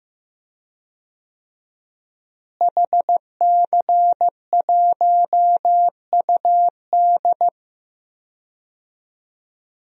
Morse code, heard as a keyed tone, about 15 wpm, 700 Hz. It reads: HC1UD
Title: DXCC-100 Call Signs - 5 Characters - Repeated 3x - 15wpm